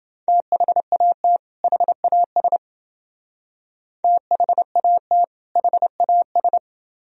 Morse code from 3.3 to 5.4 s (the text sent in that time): THAT